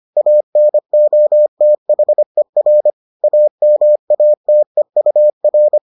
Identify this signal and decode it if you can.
Morse code, heard as a keyed tone, about 25 wpm, 600 Hz. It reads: ANOTHER AMATEUR